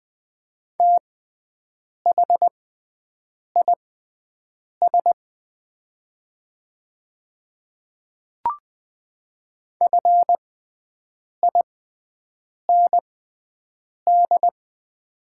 Morse code, heard as a keyed tone, about 20 wpm, spaced as for 7 wpm, 700 Hz. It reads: THIS FIND
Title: 6x Character Spacing - Top 100 Words - 20wpm